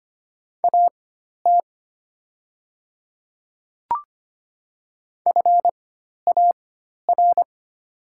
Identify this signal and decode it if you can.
Morse code, heard as a keyed tone, about 25 wpm, spaced as for 12 wpm, 700 Hz. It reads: AT FAR
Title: 4x Character Spacing - Top 500 Words - 25wpm